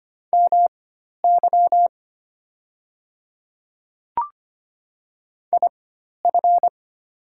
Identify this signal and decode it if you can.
Morse code, heard as a keyed tone, about 25 wpm, spaced as for 12 wpm, 690 Hz. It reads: MY IF
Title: 4x Character Spacing - Top 100 Words - 25wpm